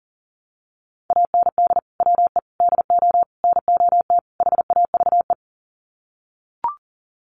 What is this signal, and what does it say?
Morse code, heard as a keyed tone, about 40 wpm, 700 Hz. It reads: AND WE DO NOT HAVE
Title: Sets of 5 Words Encore 40wpm